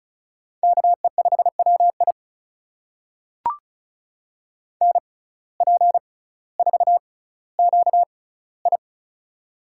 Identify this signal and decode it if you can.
Morse code, heard as a keyed tone, about 36 wpm, spaced as for 18 wpm, 700 Hz. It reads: KE5WI NP4QI
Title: USA Call Signs - 5 Characters - 6x Character Spacing - 35wpm